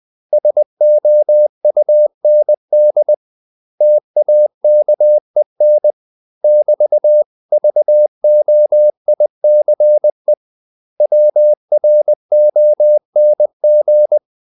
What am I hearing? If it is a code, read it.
Morse code, heard as a keyed tone, about 20 wpm, 600 Hz. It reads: SOUND TAKEN <BT>VOICE WRONG
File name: CW / Practice / 500MostCommonEnglishWords20